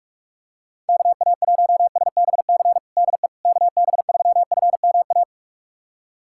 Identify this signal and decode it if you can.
Morse code, heard as a keyed tone, about 42 wpm, 690 Hz. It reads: KA1SBC DE KB3FMA
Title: USA Callsign DE Callsign - 6 Characters - 45wpm